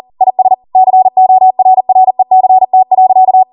Morse code, heard as a keyed tone, about 40 wpm, 750 Hz. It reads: IS CORRECT.
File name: Morse / 40WPM